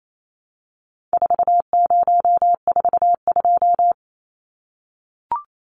Morse code, heard as a keyed tone, about 28 wpm, 700 Hz. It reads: 4042
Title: Numbers - 4-Digit 28wpm